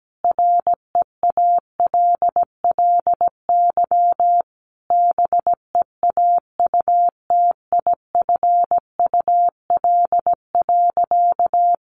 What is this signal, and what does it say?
Morse code, heard as a keyed tone, about 17 wpm, 700 Hz. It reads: REALLY BEAUTIFUL.
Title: Sentences from Top 1100 Words 17wpm